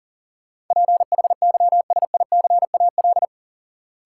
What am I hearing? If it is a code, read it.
Morse code, heard as a keyed tone, about 40 wpm, 700 Hz. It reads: PHYSICAL
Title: Sentences from Top 1000 Words - Review Part 4 - New Word Drill - 40wpm